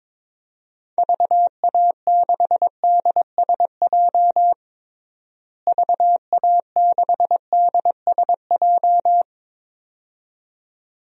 Morse code, heard as a keyed tone, about 22 wpm, 700 Hz. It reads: VA6DSJ VA6DSJ
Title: DXCC-100 Call Signs - 6 Characters - Repeated 2x - 22wpm